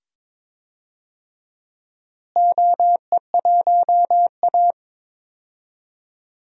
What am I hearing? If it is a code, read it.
Morse code, heard as a keyed tone, about 22 wpm, 700 Hz. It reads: OE1A
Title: DXCC-100 Call Signs - 4 Characters - Repeated 2x - 22wpm